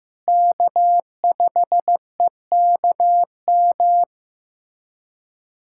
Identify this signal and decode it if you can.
Morse code, heard as a keyed tone, about 15 wpm, 700 Hz. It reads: K5EKM
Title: USA Callsign DE Callsign - 5 Characters - 15wpm